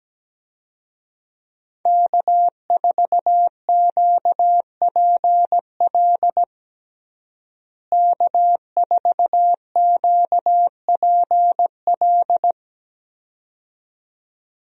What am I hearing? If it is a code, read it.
Morse code, heard as a keyed tone, about 17 wpm, 700 Hz. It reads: K4QPL K4QPL
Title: USA Call Signs - 5 Characters - Repeated 2x - 17wpm